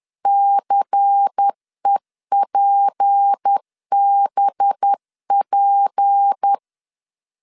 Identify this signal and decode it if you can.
Morse code, heard as a keyed tone, about 10 wpm, 800 Hz. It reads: CEPBP
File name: LEZIONI / 6 PKX0 / 6 P 10wpm 2432